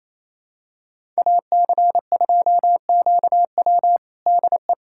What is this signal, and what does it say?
Morse code, heard as a keyed tone, about 28 wpm, 700 Hz. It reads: AC2QW DE
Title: USA Callsign DE Callsign - 5 Characters - 28wpm